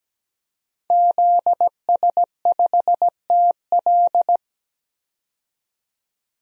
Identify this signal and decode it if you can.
Morse code, heard as a keyed tone, about 17 wpm, 700 Hz. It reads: ZS5TL
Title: DXCC-100 Call Signs - 5 Characters - 17wpm